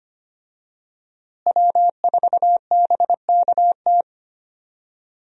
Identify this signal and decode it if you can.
Morse code, heard as a keyed tone, about 25 wpm, 700 Hz. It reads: W4BKT
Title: USA Call Signs - 5 Characters - Repeated 2x - 25wpm